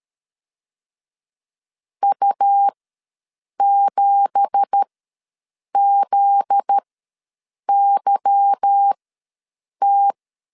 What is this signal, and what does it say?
Morse code, heard as a keyed tone, about 13 wpm, spaced as for 7 wpm, 800 Hz. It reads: U7ZYT
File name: LEZIONI / 9 2867 ws 12wpm 3431